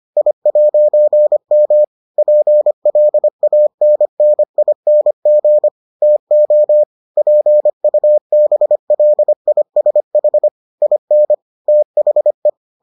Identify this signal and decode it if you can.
Morse code, heard as a keyed tone, about 25 wpm, 600 Hz. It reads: I'M PLANNING TO PUBLISH IN THE